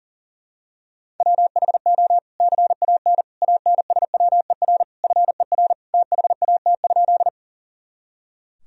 Morse code, heard as a keyed tone, about 37 wpm, 700 Hz. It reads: WHO CAN ANSWER FER THAT?
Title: Sentences from Top 300 Words and Abbreviations 40wpm